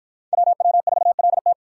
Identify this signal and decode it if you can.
Morse code, heard as a keyed tone, about 53 wpm, 700 Hz. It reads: WW4LT